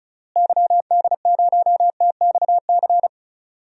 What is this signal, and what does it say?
Morse code, heard as a keyed tone, about 34 wpm, 700 Hz. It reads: YD0TXC
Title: DXCC-100 Call Signs - 6 Characters - Repeated 3x - 35wpm